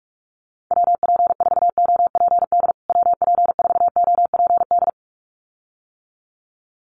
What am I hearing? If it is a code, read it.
Morse code, heard as a keyed tone, about 46 wpm, 700 Hz. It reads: WP4OPD WP4OPD
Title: USA Call Signs Repeated 2x 45wpm